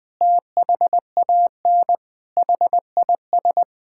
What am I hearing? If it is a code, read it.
Morse code, heard as a keyed tone, about 20 wpm, 700 Hz. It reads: THAN HIS
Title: Difficult Sentences with Hints - 20wpm